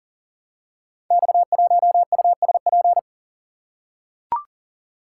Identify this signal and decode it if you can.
Morse code, heard as a keyed tone, about 40 wpm, 690 Hz. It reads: X1USP
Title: Code Groups - 5 Letter:Numbers - 40wpm